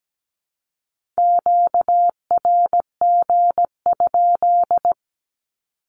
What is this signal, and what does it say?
Morse code, heard as a keyed tone, about 17 wpm, 700 Hz. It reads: QRG?
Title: Q-Codes 17wpm